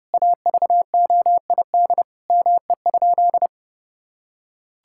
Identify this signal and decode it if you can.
Morse code, heard as a keyed tone, about 30 wpm, 700 Hz. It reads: AVOID ME?
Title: Sentences from Top 1000 Words - Review Part 6 - 30wpm